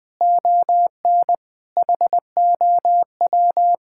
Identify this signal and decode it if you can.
Morse code, heard as a keyed tone, about 21 wpm, 700 Hz. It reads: ON HOW
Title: Sentences with 7-Word Phrases - 20wpm